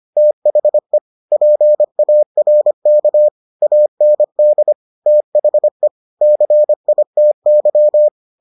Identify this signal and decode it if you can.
Morse code, heard as a keyed tone, about 25 wpm, 600 Hz. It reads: THE PARK AND THE CITY